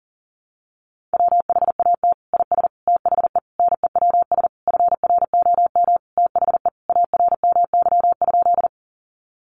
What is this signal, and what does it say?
Morse code, heard as a keyed tone, about 40 wpm, 700 Hz. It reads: WHAT IS THE NEWS FROM THE ARMY?
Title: Sentences from Top 800 Words 40wpm